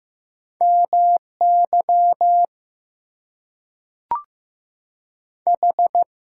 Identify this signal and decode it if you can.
Morse code, heard as a keyed tone, about 15 wpm, 700 Hz. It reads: MY H
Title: Two Letter Words - 15wpm